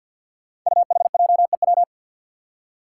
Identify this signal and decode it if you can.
Morse code, heard as a keyed tone, about 52 wpm, 700 Hz. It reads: UHJEW
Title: Five Challenging Letter-Number Combinations 50wpm